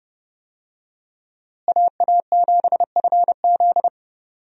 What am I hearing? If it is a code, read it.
Morse code, heard as a keyed tone, about 32 wpm, 700 Hz. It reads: AA7FZ
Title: USA Call Signs - 5 Characters - Repeated 2x - 30wpm